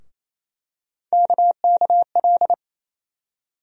Morse code, heard as a keyed tone, about 28 wpm, 700 Hz. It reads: KKL
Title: USA Call Sign Postfixes - 3 Letters 28wpm